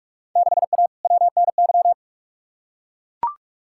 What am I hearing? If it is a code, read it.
Morse code, heard as a keyed tone, about 43 wpm, 700 Hz. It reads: 6A WNY
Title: Field Day Exchanges - 45wpm